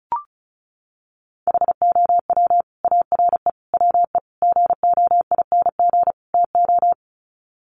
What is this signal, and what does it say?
Morse code, heard as a keyed tone, about 34 wpm, 700 Hz. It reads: HOW ARE WE GOING TO
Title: Sets of 5 Words Encore 35wpm